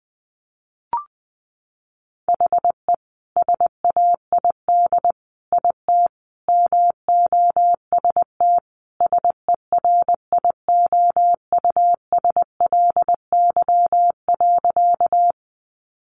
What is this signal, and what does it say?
Morse code, heard as a keyed tone, about 20 wpm, 700 Hz. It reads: HE SAID IT MOST SERIOUSLY.
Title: Sentences from Top 1200 Words 20wpm